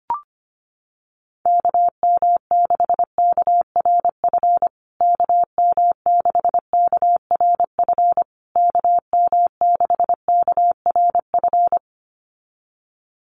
Morse code, heard as a keyed tone, about 25 wpm, 700 Hz. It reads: KM6KRF KM6KRF KM6KRF
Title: USA Call Signs Repeated 3x 25wpm